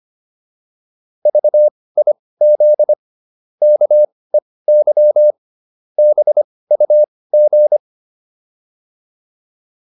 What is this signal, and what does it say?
Morse code, heard as a keyed tone, about 25 wpm, spaced as for 18 wpm, 600 Hz. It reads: VIZ KEY BUG